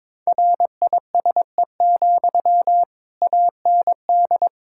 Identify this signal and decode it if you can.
Morse code, heard as a keyed tone, about 22 wpm, 700 Hz. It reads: RISE, AND
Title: Sentences from Top 1000 Words - Review Part 6 - 22wpm